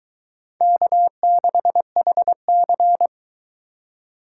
Morse code, heard as a keyed tone, about 23 wpm, 700 Hz. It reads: K6HC